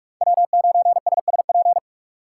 Morse code, heard as a keyed tone, about 44 wpm, 700 Hz. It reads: W9SSP